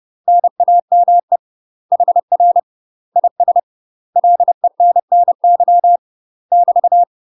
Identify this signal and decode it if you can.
Morse code, heard as a keyed tone, about 30 wpm, 700 Hz. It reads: NAME HR IS LENNY <BT>
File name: UK MASTER QSOs TRAINING / 30WPM / UK MASTER QSOs 0005